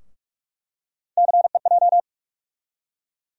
Morse code, heard as a keyed tone, about 42 wpm, 700 Hz. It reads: CEJ
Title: USA Call Sign Postfixes - 3 Letters 45wpm